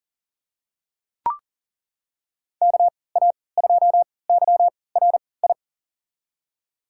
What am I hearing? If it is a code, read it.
Morse code, heard as a keyed tone, about 40 wpm, spaced as for 23 wpm, 700 Hz. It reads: KA2YRI
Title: USA Call Signs - 6 Characters - 3x Character Spacing - 40wpm